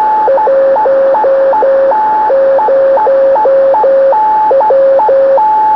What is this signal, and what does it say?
Morse code, heard as a keyed tone, about 12 wpm, 540 Hz, with radio noise. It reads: 10W
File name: gm8rbr b-4m